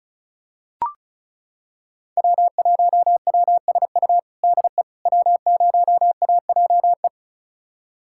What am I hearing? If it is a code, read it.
Morse code, heard as a keyed tone, about 34 wpm, 700 Hz. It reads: W1WSU DE W0AJE
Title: USA Callsign DE Callsign - 5 Characters - 35wpm